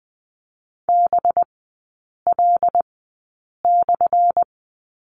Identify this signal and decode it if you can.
Morse code, heard as a keyed tone, about 20 wpm, spaced as for 8 wpm, 700 Hz. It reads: BL/
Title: 184 Lesson 30e - Letter b - Three Characters 20wpm